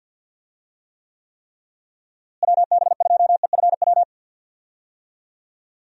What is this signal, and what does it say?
Morse code, heard as a keyed tone, about 51 wpm, 700 Hz. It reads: WB2EFW
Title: USA Call Signs - 6 Characters - Repeated 3x - 50wpm